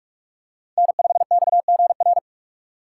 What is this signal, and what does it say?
Morse code, heard as a keyed tone, about 43 wpm, 700 Hz. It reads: N5XGR
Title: USA Call Signs - 5 Characters - 6x Character Spacing - 45wpm